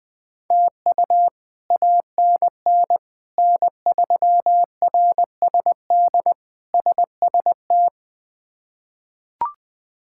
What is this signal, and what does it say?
Morse code, heard as a keyed tone, about 21 wpm, 700 Hz. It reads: TU ANN N3RSD SST
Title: K1USN SST QSO Practice - 20wpm